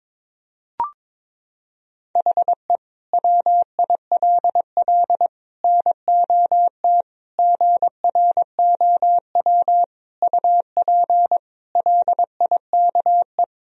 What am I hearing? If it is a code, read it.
Morse code, heard as a keyed tone, about 22 wpm, 700 Hz. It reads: HE WILL NOT GROW UP LIKE
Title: Sentences from Top 1000 Words - Review Part 1 - 22wpm